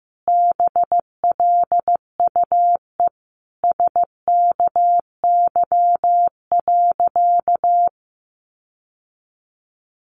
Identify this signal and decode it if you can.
Morse code, heard as a keyed tone, about 15 wpm, 700 Hz. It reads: BLUE SKY.
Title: Sentences from Top 1300 Words 15wpm